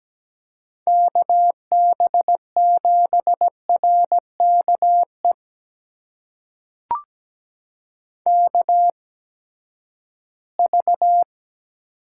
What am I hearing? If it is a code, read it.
Morse code, heard as a keyed tone, about 17 wpm, spaced as for 12 wpm, 700 Hz. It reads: KB7RKE KV
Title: USA Call Signs - 8x Character Spacing - 17wpm